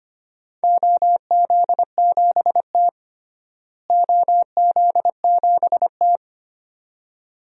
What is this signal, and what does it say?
Morse code, heard as a keyed tone, about 25 wpm, 700 Hz. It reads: OZ7T OZ7T
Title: DXCC-100 Call Signs - 4 Characters - Repeated 2x - 25wpm